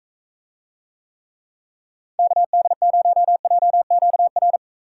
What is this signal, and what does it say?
Morse code, heard as a keyed tone, about 45 wpm, 700 Hz. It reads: KD0JQR